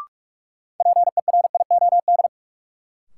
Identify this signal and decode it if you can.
Morse code, heard as a keyed tone, about 46 wpm, 700 Hz. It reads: PERIOD
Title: Sentences from Top 800 Words - New Word Drill - 45wpm